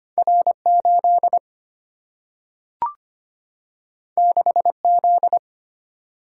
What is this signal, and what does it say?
Morse code, heard as a keyed tone, about 25 wpm, 700 Hz. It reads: R 8 6 Z
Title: Single-Letter Single-Number 25wpm